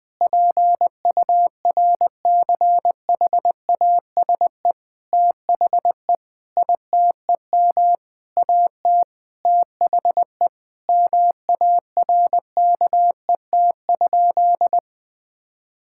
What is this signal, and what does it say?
Morse code, heard as a keyed tone, about 20 wpm, 700 Hz. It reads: PURCHASE THE ITEM AT THE MARKET?
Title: Sentences from Top 1000 Words - Review Part 6 - 20wpm